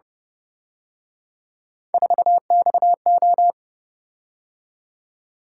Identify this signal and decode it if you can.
Morse code, heard as a keyed tone, about 29 wpm, 700 Hz. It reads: 4XO
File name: Speed-Racing - Three Letter-Number Combinations - 25wpm